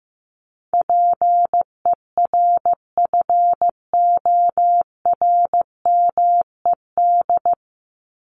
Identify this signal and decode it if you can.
Morse code, heard as a keyed tone, about 15 wpm, 700 Hz. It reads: PERFORMED
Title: ED Suffix - Top 100 Words with Base Words 15wpm